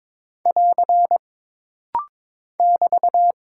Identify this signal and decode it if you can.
Morse code, heard as a keyed tone, about 22 wpm, 700 Hz. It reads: <AR> <BT>
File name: practice / 22wpm / Single Punctuation-Prosign - Rapid-Fire - 22wpm